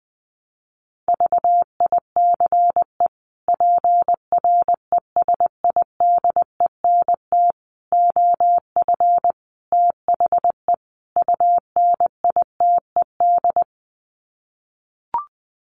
Morse code, heard as a keyed tone, about 20 wpm, 700 Hz. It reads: VICE PRESIDENT OF THE UNITED
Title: Sets of 5 Words Encore2 20wpm